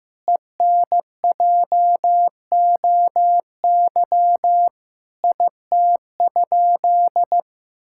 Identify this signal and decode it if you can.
Morse code, heard as a keyed tone, about 16 wpm, 700 Hz. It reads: ENJOY IT?
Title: Sentences from Top 1000 Words - Review Part 5 - 15wpm